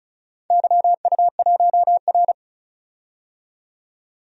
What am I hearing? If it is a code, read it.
Morse code, heard as a keyed tone, about 34 wpm, 700 Hz. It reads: YU1R
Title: DXCC-100 Call Signs - 4 Characters - 35wpm